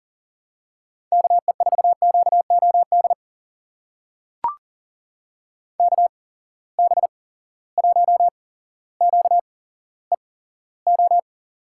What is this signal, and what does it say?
Morse code, heard as a keyed tone, about 40 wpm, spaced as for 17 wpm, 700 Hz. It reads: KE4QOD KB1QEO
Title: USA Call Signs - 6 Characters - 8x Character Spacing - 40wpm